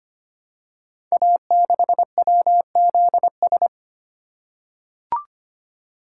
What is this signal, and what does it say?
Morse code, heard as a keyed tone, about 25 wpm, 700 Hz. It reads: A6WZS
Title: Code Groups - 5 Letter:Numbers - 25wpm